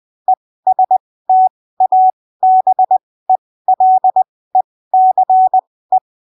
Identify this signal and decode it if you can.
Morse code, heard as a keyed tone, about 20 wpm, spaced as for 15 wpm, 750 Hz. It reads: ESTABELECE
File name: arquivos / texto / texto 12 20ppm